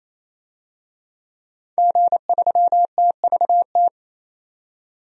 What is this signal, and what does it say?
Morse code, heard as a keyed tone, about 28 wpm, 700 Hz. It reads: G3TVT
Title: DXCC-100 Call Signs - 5 Characters - 28wpm